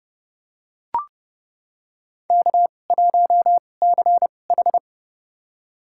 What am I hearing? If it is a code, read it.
Morse code, heard as a keyed tone, about 30 wpm, spaced as for 22 wpm, 700 Hz. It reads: K1CH
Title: USA Call Signs - 4 Characters - 2x Character Spacing - 30wpm